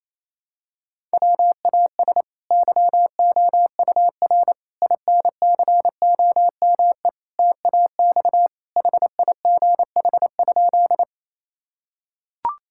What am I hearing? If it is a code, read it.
Morse code, heard as a keyed tone, about 28 wpm, 700 Hz. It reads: WAS YOUR INCOME TAX HIGH?
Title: Sentences from Top 1000 Words - Review Part 5 - 28wpm